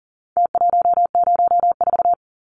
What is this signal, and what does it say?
Morse code, heard as a keyed tone, about 40 wpm, 700 Hz. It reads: T104
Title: Numbers - Contest Style - 4-Digit 40wpm